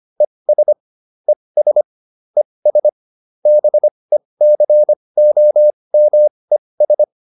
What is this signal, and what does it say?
Morse code, heard as a keyed tone, about 25 wpm, spaced as for 20 wpm, 600 Hz. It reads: ES ES ES BECOMES
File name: ES7 20